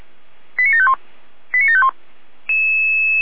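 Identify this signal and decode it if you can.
Five-tone selective call (ZVEI), heard as quick runs of five tones, tones of about 70 ms. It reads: 89751 89751
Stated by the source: Five-tone ZVEI selective call received by SDR.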